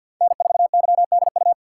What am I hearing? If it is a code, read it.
Morse code, heard as a keyed tone, about 53 wpm, 700 Hz. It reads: N4YDU
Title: K1USN SST QSO Practice - 50wpm